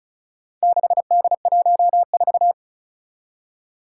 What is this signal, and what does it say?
Morse code, heard as a keyed tone, about 36 wpm, 700 Hz. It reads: 6D14
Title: Four Challenging Letter-Number Combinations 35wpm